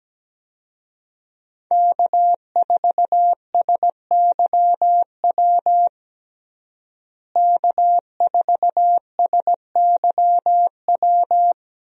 Morse code, heard as a keyed tone, about 17 wpm, 700 Hz. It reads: K4SYW K4SYW
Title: USA Call Signs - 5 Characters - Repeated 3x - 17wpm